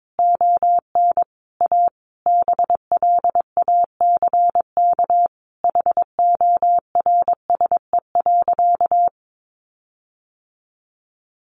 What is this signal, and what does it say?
Morse code, heard as a keyed tone, about 22 wpm, 700 Hz. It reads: ON A BLACK HORSE.
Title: Sentences from Top 1500 Words 22wpm